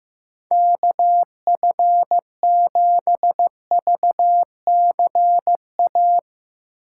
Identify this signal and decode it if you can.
Morse code, heard as a keyed tone, about 15 wpm, 700 Hz. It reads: KF7VCA
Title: USA Call Signs - 6 Characters - 8x Character Spacing - 15wpm